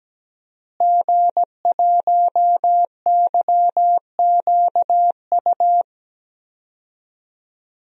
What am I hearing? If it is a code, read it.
Morse code, heard as a keyed tone, about 17 wpm, 700 Hz. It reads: G1YQU
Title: DXCC-100 Call Signs - 5 Characters - Repeated 2x - 17wpm